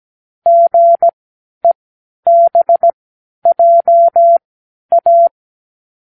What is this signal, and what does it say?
Morse code, heard as a keyed tone, about 17 wpm, spaced as for 10 wpm, 680 Hz. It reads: GEBJA
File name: k7qo / Lesson18(A-L)